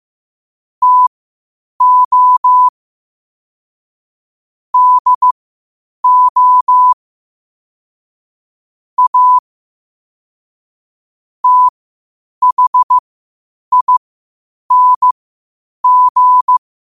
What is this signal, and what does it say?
Morse code, heard as a keyed tone, about 15 wpm, spaced as for 8 wpm, 1000 Hz. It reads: TO DO A THING
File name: qotdfiles / 8wpm